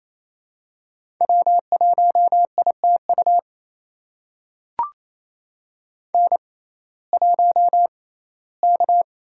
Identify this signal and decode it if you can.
Morse code, heard as a keyed tone, about 27 wpm, spaced as for 17 wpm, 700 Hz. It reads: W1ITU N1K
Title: USA Call Signs - 5 Characters - 6x Character Spacing - 28wpm